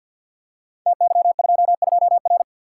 Morse code, heard as a keyed tone, about 50 wpm, 700 Hz. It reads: TY22R
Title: Code Groups - 5 Letter:Numbers - 50wpm